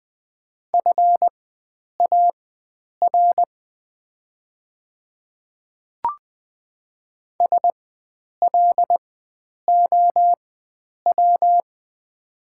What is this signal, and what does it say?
Morse code, heard as a keyed tone, about 20 wpm, spaced as for 9 wpm, 700 Hz. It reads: FAR SLOW
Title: 4x Character Spacing - Top 500 Words - 20wpm